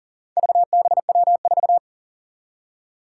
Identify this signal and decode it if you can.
Morse code, heard as a keyed tone, about 40 wpm, 700 Hz. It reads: VBW4
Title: Four Challenging Letter-Number Combinations 40wpm